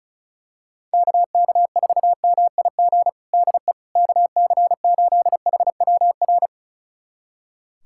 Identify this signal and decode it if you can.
Morse code, heard as a keyed tone, about 36 wpm, 700 Hz. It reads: KK4MIG DE KC8HWR
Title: USA Callsign DE Callsign 35wpm